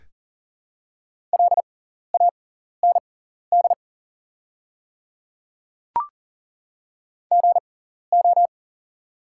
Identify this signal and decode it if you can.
Morse code, heard as a keyed tone, about 40 wpm, spaced as for 14 wpm, 700 Hz. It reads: LAND GO
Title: 6x Character Spacing - Top 300 Words - 40wpm